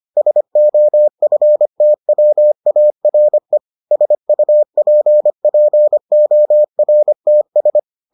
Morse code, heard as a keyed tone, about 25 wpm, 600 Hz. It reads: SOFTWARE SUPPORTS